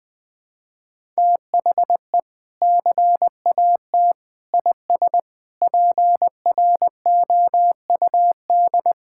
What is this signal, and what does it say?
Morse code, heard as a keyed tone, about 21 wpm, 700 Hz. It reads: THE CAT IS PROUD
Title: Sentences with 2-Word Phrases - 20wpm